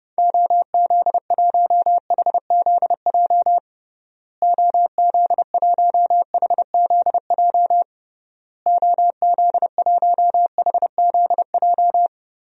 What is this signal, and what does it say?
Morse code, heard as a keyed tone, about 30 wpm, 700 Hz. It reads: OZ1HZJ OZ1HZJ OZ1HZJ
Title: DXCC-100 Call Signs - 6 Characters - Repeated 3x - 30wpm